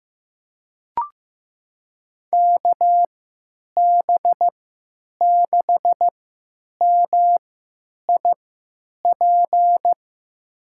Morse code, heard as a keyed tone, about 15 wpm, spaced as for 9 wpm, 700 Hz. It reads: KB6MIP
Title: USA Call Signs - 6 Characters - 3x Character Spacing - 15wpm